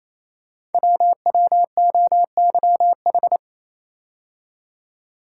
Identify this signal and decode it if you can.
Morse code, heard as a keyed tone, about 29 wpm, 700 Hz. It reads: WWOYH
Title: Code Groups - 5 Letters - 28wpm